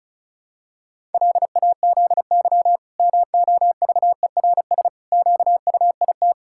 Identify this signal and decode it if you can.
Morse code, heard as a keyed tone, about 34 wpm, 700 Hz. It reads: LAZY MOVERS QUIT
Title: Pangram Challenge v2 - 35wpm